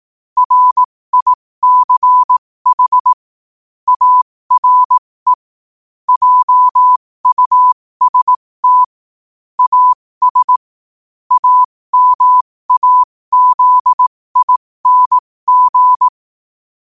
Morse code, heard as a keyed tone, about 18 wpm, spaced as for 15 wpm, 1000 Hz. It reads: RICH ARE JUST AS AMAZING
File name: qotdfiles / 15wpm